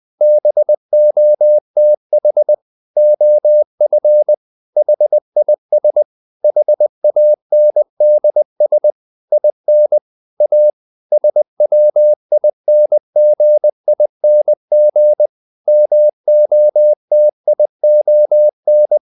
Morse code, heard as a keyed tone, about 20 wpm, 600 Hz. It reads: BOTH OF HIS HANDS IN A SWINGING MOTION
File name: the old man and the sea part 2 20wpm